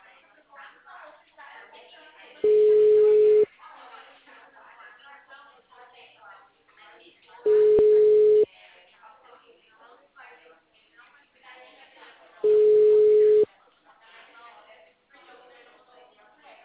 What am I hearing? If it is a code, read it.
Ringback tone (Europe, 425 Hz), with speech.
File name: out- -2009- - - .232